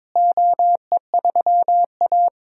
Morse code, heard as a keyed tone, about 22 wpm, 700 Hz. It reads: OE3A
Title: DXCC-100 Call Signs - 4 Characters - Repeated 3x - 22wpm